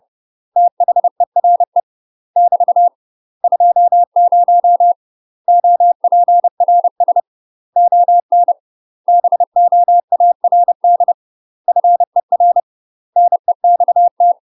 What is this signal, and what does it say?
Morse code, heard as a keyed tone, about 30 wpm, 700 Hz. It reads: THERE <BT> 20 OPRS ON BOARD FER NEXT
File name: UK MASTER QSOs TRAINING / 30WPM / UK MASTER QSOs 0018